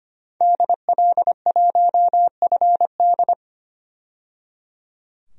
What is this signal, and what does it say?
Morse code, heard as a keyed tone, about 25 wpm, 700 Hz. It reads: DL1FD
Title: DXCC-100 Call Signs - 5 Characters - 25wpm